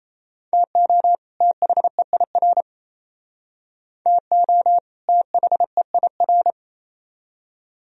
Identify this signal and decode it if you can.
Morse code, heard as a keyed tone, about 32 wpm, 700 Hz. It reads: TO THEIR TO THEIR
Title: Speed-Racing - Sets of 2 Words 28wpm